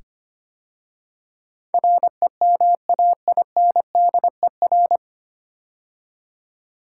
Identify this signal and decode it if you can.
Morse code, heard as a keyed tone, about 25 wpm, 700 Hz. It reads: REMAINDER
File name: practice / 25wpm / 9 Letter Words - First 3 Letters Given - 25wpm